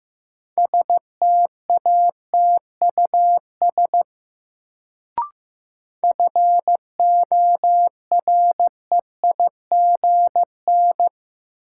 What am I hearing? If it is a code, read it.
Morse code, heard as a keyed tone, about 15 wpm, 700 Hz. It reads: STATUS FOREIGN
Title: Sentences from Top 1400 Words - New Word Drill - 15wpm